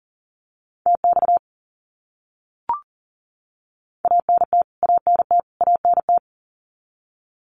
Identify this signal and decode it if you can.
Morse code, heard as a keyed tone, about 40 wpm, 700 Hz. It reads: TX ANT ANT ANT
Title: Abbreviations Repeated 3x - 40wpm